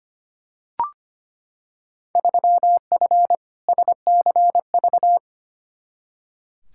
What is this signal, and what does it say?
Morse code, heard as a keyed tone, about 25 wpm, 700 Hz. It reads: 3F SCV
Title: Field Day Exchanges - 25wpm